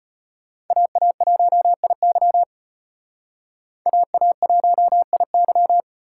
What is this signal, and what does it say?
Morse code, heard as a keyed tone, about 33 wpm, 700 Hz. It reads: AA1IY AA1IY